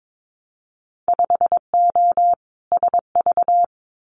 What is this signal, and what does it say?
Morse code, heard as a keyed tone, about 22 wpm, 700 Hz. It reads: 5O SV
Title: Winter Field Day Exchanges Encore 22wpm